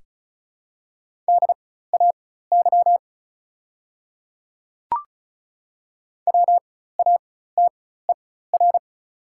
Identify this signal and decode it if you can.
Morse code, heard as a keyed tone, about 36 wpm, spaced as for 16 wpm, 700 Hz. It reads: DAY WATER
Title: 4x Character Spacing - Top 100 Words - 35wpm